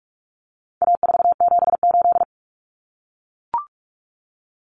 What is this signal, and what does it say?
Morse code, heard as a keyed tone, about 44 wpm, 700 Hz. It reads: A478
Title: Common Cut Numbers - 4-Digit 45wpm